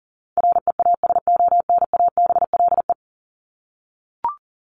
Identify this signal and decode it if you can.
Morse code, heard as a keyed tone, about 40 wpm, 700 Hz. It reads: REASONABLE
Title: Top 1501-1750 Words 40wpm